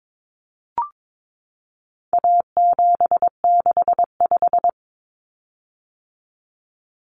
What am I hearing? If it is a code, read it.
Morse code, heard as a keyed tone, about 22 wpm, 700 Hz. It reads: A765
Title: Common Cut Numbers - 4-Digit 22wpm